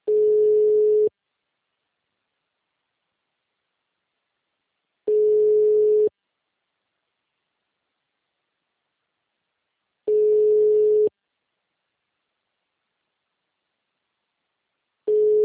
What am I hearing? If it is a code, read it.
Ringback tone (Europe, 425 Hz).
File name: out- -2002- - - .19406